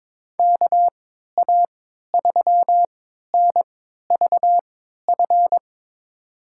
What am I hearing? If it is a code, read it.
Morse code, heard as a keyed tone, about 22 wpm, spaced as for 12 wpm, 700 Hz. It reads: KA3NVF
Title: USA Call Signs - 6 Characters - 3x Character Spacing - 22wpm